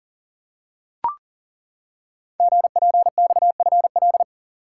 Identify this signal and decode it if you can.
Morse code, heard as a keyed tone, about 40 wpm, 700 Hz. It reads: GPXFL
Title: Code Groups - 5 Letters - 40wpm